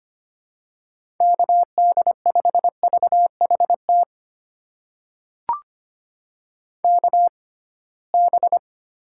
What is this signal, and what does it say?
Morse code, heard as a keyed tone, about 25 wpm, spaced as for 19 wpm, 700 Hz. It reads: KD5VHT KB
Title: USA Call Signs - 6 Characters - 6x Character Spacing - 25wpm